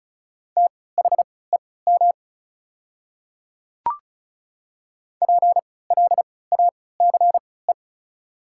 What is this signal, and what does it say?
Morse code, heard as a keyed tone, about 36 wpm, spaced as for 20 wpm, 690 Hz. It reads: THEM PLACE
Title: 3x Character Spacing - Top 300 Words - 35wpm